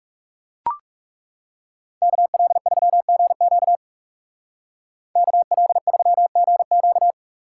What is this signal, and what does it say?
Morse code, heard as a keyed tone, about 40 wpm, 700 Hz. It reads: KL3GQ KL3GQ